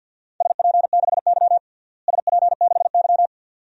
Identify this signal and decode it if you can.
Morse code, heard as a keyed tone, about 46 wpm, 700 Hz. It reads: SP6Y SP6Y
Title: DXCC-100 Call Signs - 4 Characters - Repeated 2x - 50wpm